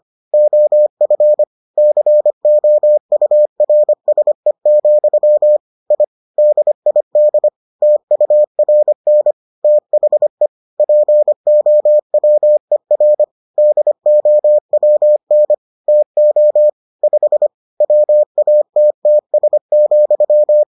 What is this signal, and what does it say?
Morse code, heard as a keyed tone, about 25 wpm, 600 Hz. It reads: OF COURSE, I DID TURN THE POWER DOWN TO 5 WATTS,